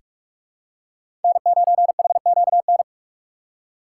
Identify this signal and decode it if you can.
Morse code, heard as a keyed tone, about 43 wpm, 700 Hz. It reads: N9HQN
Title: USA Call Signs - 5 Characters - 2x Character Spacing - 45wpm